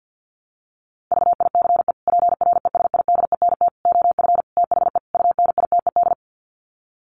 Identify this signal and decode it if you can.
Morse code, heard as a keyed tone, about 50 wpm, 700 Hz. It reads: VICE PRESIDENT OF THE UNITED
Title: Sets of 5 Words Encore2 50wpm